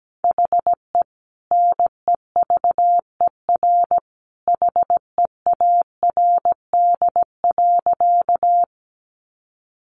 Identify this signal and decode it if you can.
Morse code, heard as a keyed tone, about 17 wpm, 700 Hz. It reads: HE NEVER HEARD.
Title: Sentences from Top 400 Words 17wpm